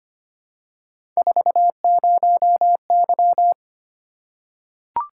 Morse code, heard as a keyed tone, about 25 wpm, 700 Hz. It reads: 40Y
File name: Three Letter-Number Combinations - 25wpm